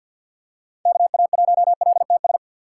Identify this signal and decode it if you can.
Morse code, heard as a keyed tone, about 52 wpm, 700 Hz. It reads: KA1LTS
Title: USA Call Signs - 3x Character Spacing - 50wpm